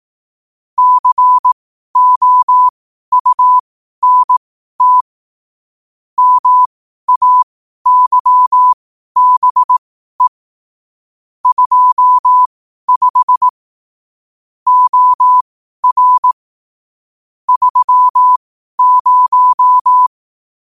Morse code, heard as a keyed tone, about 18 wpm, spaced as for 12 wpm, 1000 Hz. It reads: COUNT MAYBE 25 OR 30
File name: qotdfiles / 12wpm